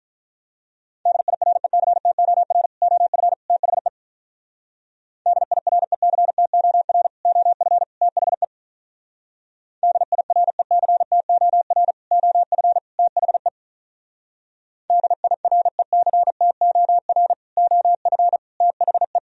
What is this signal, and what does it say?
Morse code, heard as a keyed tone, about 45 wpm, 700 Hz. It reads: DIRECTOR OF THE DIRECTOR OF THE DIRECTOR OF THE DIRECTOR OF THE